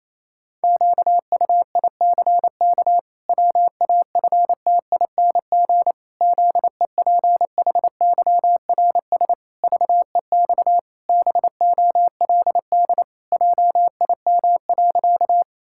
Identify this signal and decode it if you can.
Morse code, heard as a keyed tone, about 28 wpm, 700 Hz. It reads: QUICK WAFTING ZEPHYRS VEX BOLD JIM.
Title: Pangram Challenge v2 - 28wpm